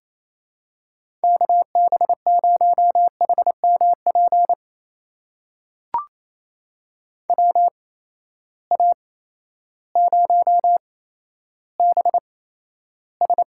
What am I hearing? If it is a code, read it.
Morse code, heard as a keyed tone, about 28 wpm, spaced as for 13 wpm, 700 Hz. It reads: KB0HMP WA0BS
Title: USA Call Signs - 6 Characters - 8x Character Spacing - 28wpm